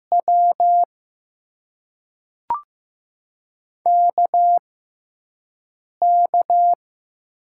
Morse code, heard as a keyed tone, about 15 wpm, spaced as for 5 wpm, 700 Hz. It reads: W KK
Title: USA Call Signs - 6 Characters - 6x Character Spacing - 15wpm